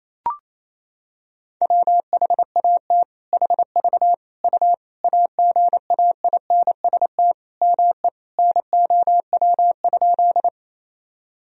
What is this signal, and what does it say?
Morse code, heard as a keyed tone, about 28 wpm, 700 Hz. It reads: WHAT HV U AGAINST ME NOW?
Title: Sentences from Top 400 Words and Abbreviations 28wpm